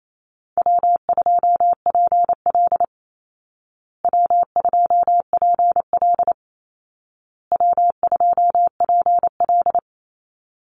Morse code, heard as a keyed tone, about 28 wpm, 700 Hz. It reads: W2PL W2PL W2PL
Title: Easy Call Signs Repeated 3x - 28wpm